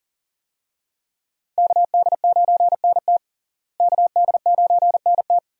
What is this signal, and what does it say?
Morse code, heard as a keyed tone, about 40 wpm, 700 Hz. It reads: KD9NT KD9NT
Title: USA Call Signs - 5 Characters - Repeated 2x - 40wpm